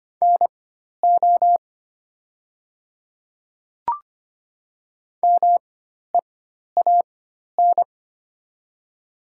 Morse code, heard as a keyed tone, about 25 wpm, spaced as for 12 wpm, 700 Hz. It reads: NO MEAN
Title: 4x Character Spacing - Top 300 Words - 25wpm